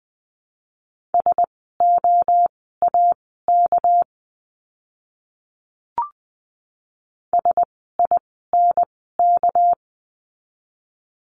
Morse code, heard as a keyed tone, about 20 wpm, spaced as for 14 wpm, 700 Hz. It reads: SOAK SINK
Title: 199 Lesson 32f - Letter k - Words 20wpm